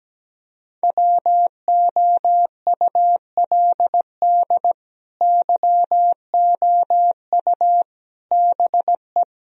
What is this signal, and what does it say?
Morse code, heard as a keyed tone, about 17 wpm, 700 Hz. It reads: WOULD YOU BE
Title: Sentences from Top 1000 Words - Review Part 2 - 17wpm